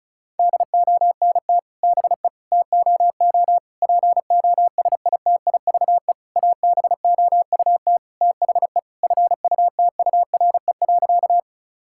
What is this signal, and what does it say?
Morse code, heard as a keyed tone, about 36 wpm, 700 Hz. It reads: DONT BE TOO POSITIVE ABOUT THE FUTURE.
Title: Sentences from Top 1000 Words - Review Part 5 - 35wpm